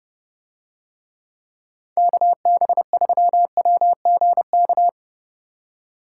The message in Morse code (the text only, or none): KB3WGK